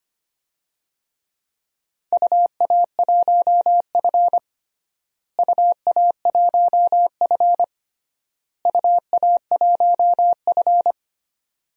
Morse code, heard as a keyed tone, about 25 wpm, 700 Hz. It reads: UA1F UA1F UA1F
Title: DXCC-100 Call Signs - 4 Characters - Repeated 3x - 25wpm